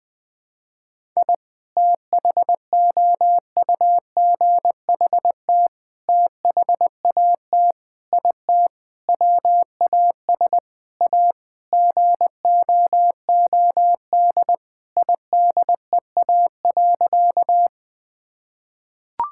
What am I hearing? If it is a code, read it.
Morse code, heard as a keyed tone, about 20 wpm, 700 Hz. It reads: I THOUGHT THAT IT WAS A GOOD IDEA.
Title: Sentences with 4-Word Phrases - Spoken Beginning - 20wpm